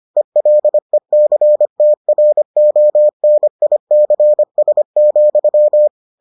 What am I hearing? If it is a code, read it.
Morse code, heard as a keyed tone, about 25 wpm, 600 Hz. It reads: ELECTRONICS,